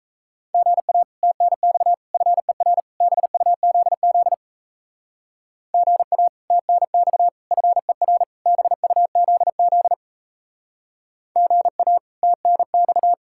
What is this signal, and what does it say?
Morse code, heard as a keyed tone, about 34 wpm, 700 Hz. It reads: GA TNX FER BUZZ GA TNX FER BUZZ GA TNX
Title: Speed-Racing Common QSO Phrases Encore - 28wpm